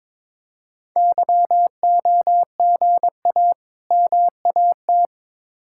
Morse code, heard as a keyed tone, about 22 wpm, 700 Hz. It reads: YOGA MAT
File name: practice / 22wpm / Difficult Words with Hints - 22wpm